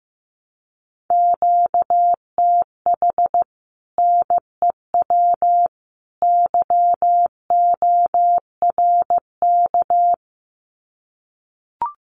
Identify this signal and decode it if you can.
Morse code, heard as a keyed tone, about 15 wpm, 700 Hz. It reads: QTH NEW YORK
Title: Common QSO Phrases 15wpm